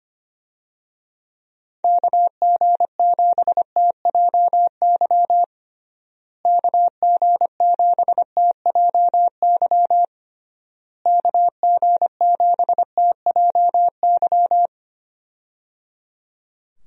Morse code, heard as a keyed tone, about 25 wpm, 700 Hz. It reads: KG7TJY KG7TJY KG7TJY
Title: USA Call Signs - 6 Characters - Repeated 3x - 25wpm